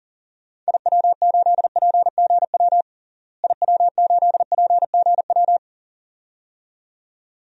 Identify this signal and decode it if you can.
Morse code, heard as a keyed tone, about 37 wpm, 700 Hz. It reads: IW8PGW IW8PGW
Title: DXCC-100 Call Signs - 6 Characters - Repeated 2x - 40wpm